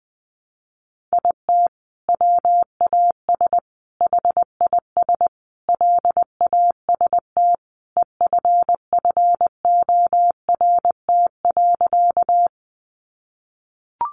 Morse code, heard as a keyed tone, about 20 wpm, 700 Hz. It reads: IT WAS HIS LAST EFFORT.
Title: Sentences from Top 1200 Words 20wpm